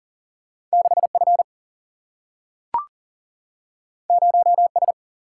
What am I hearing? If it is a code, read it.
Morse code, heard as a keyed tone, about 40 wpm, 700 Hz. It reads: 6 F 0 S
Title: Single-Letter Single-Number 40wpm